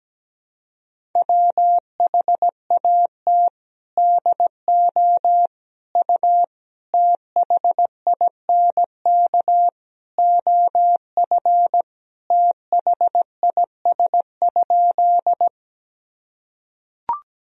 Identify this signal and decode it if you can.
Morse code, heard as a keyed tone, about 17 wpm, 700 Hz. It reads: WHAT DO U THINK OF THIS?
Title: Sentences from Top 200 Words and Abbreviations 17wpm